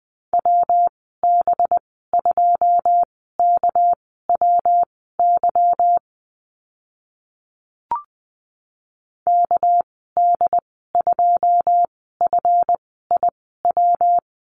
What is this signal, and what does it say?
Morse code, heard as a keyed tone, about 20 wpm, spaced as for 14 wpm, 700 Hz. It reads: WB2KWY KD2FIW
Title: 207 Lesson 33g - Letter j - Call Signs 20wpm